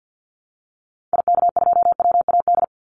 Morse code, heard as a keyed tone, about 48 wpm, 700 Hz. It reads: IK2WAD
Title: Call Signs Repeated 3x - 50wpm